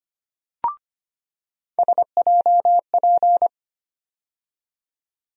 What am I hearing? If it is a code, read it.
Morse code, heard as a keyed tone, about 25 wpm, 700 Hz. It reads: SJP
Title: USA Call Sign Postfixes - 3 Letters 25wpm